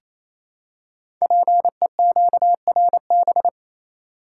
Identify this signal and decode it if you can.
Morse code, heard as a keyed tone, about 28 wpm, 690 Hz. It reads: PEQRB
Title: Code Groups - 5 Letters - 28wpm